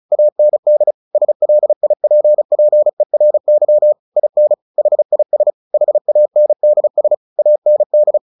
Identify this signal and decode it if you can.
Morse code, heard as a keyed tone, about 34 wpm, 600 Hz. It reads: AND SLIPPERY IN HIS HANDS AND
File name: the old man and the sea 2 35wpm